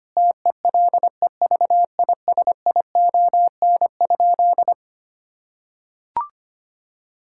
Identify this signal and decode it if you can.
Morse code, heard as a keyed tone, about 25 wpm, 700 Hz. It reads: TELEVISION?
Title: Sentences from Top 1000 Words - Review Part 5 - 25wpm